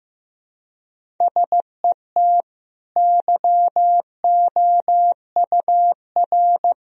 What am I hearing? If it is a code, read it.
Morse code, heard as a keyed tone, about 15 wpm, 700 Hz. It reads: SET YOUR
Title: Sentences from Top 1000 Words - Review Part 1 - 15wpm